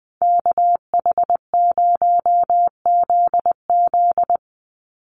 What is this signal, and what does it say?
Morse code, heard as a keyed tone, about 20 wpm, 700 Hz. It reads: KH0ZZ
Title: Call Signs - 20wpm